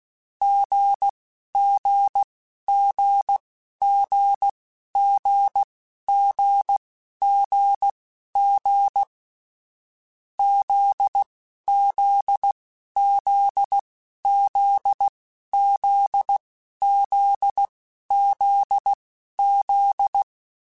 Morse code, heard as a keyed tone, about 16 wpm, spaced as for 11 wpm, 800 Hz. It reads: GGGGGGGG ZZZZZZZZ
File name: Difficult Combos / Difficult Combos-mp3s / GZ GZ7 @10